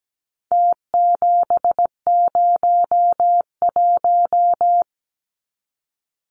Morse code, heard as a keyed tone, about 17 wpm, 700 Hz. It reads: T701
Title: Numbers - Contest Style - 4-Digit 17wpm